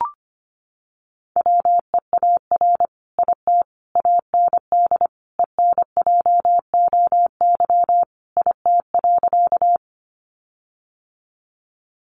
Morse code, heard as a keyed tone, about 25 wpm, 700 Hz. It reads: WEAR IT AND ENJOY IT.
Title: Sentences from Top 1500 Words 25wpm